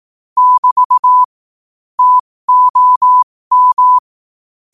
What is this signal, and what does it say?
Morse code, heard as a keyed tone, about 18 wpm, spaced as for 15 wpm, 1000 Hz. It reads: <BT> TOM
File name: qotdfiles / 15wpm